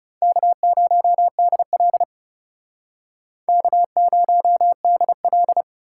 Morse code, heard as a keyed tone, about 39 wpm, 700 Hz. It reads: K0DL K0DL